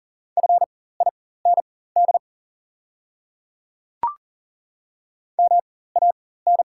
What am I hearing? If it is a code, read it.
Morse code, heard as a keyed tone, about 40 wpm, spaced as for 19 wpm, 690 Hz. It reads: FIND MAN
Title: 4x Character Spacing - Top 200 Words - 40wpm